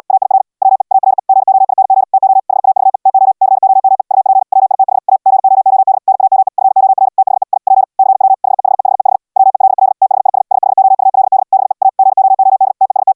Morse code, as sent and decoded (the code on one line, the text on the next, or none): .- .. .-. / .-- . / .- .-. / --. --- .. -. --. / - --- / .... .- ...- . / - --- / -.-. --- -- . / ..- .--. / .-- .. - .... / .- / --. --- --- -.. / -. .- -- . / ..-. --- .-. / - .... / .. / .--. / -.-. .-- / .... .... .... .... / .-- . .-.. .-.. / - .... .. ... / ... .... --- ..- .-.. -.. / -... / .- / --. --- --- -.. / - . ... -
AIR WE ARE GOING TO HAVE TO COME UP WITH A GOOD NAME FOR THE I P CW HEHEHEH WELL THIS SHOULD BE A GOOD TEST